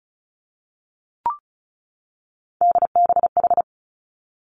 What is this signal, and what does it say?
Morse code, heard as a keyed tone, about 36 wpm, 700 Hz. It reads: DBH
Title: Three Challenging Letter-Number Combinations 35wpm